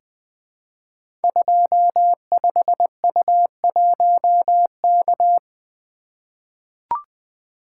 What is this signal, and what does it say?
Morse code, heard as a keyed tone, about 20 wpm, 700 Hz. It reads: 25U1K
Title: Code Groups - 5 Letter:Numbers - 20wpm